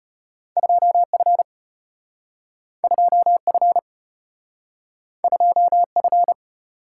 Morse code, heard as a keyed tone, about 32 wpm, 700 Hz. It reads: 2F 2F 2F